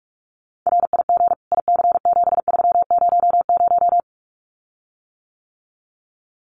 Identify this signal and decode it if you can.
Morse code, heard as a keyed tone, about 46 wpm, 700 Hz. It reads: RIG IC7300
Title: Common QSO Phrases Encore 45wpm